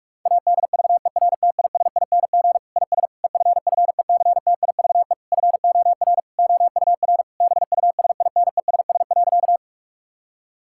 Morse code, heard as a keyed tone, about 46 wpm, 700 Hz. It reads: ADVERTISING IS EFFECTIVE FOR OUR BUSINESS.
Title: Sentences from Top 1000 Words - Review Part 6 - 45wpm